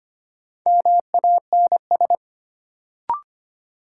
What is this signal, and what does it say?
Morse code, heard as a keyed tone, about 25 wpm, 700 Hz. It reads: MANS
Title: Sentences from Top 1000 Words - Review Part 6 - New Word Drill - 25wpm